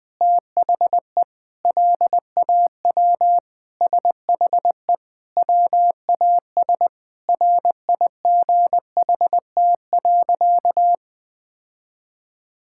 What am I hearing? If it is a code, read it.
Morse code, heard as a keyed tone, about 20 wpm, 700 Hz. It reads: THE LAW SHE WAS RIGHT.
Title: Sentences with 4-Word Phrases - Spoken Beginning - 20wpm